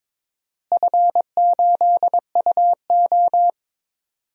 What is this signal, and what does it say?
Morse code, heard as a keyed tone, about 22 wpm, 700 Hz. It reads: F8UO
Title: DXCC-100 Call Signs - 4 Characters - Repeated 2x - 22wpm